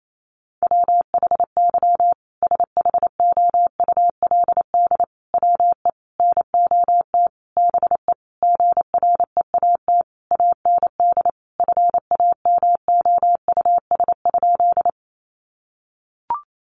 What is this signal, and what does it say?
Morse code, heard as a keyed tone, about 28 wpm, 700 Hz. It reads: WHY SHOULD WE NOT BE GREAT AND FAMOUS?
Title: Sentences from Top 1500 Words 28wpm